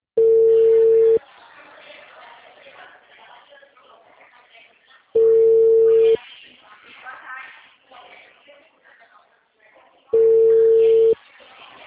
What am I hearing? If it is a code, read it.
Ringback tone (China, 450 Hz), with speech.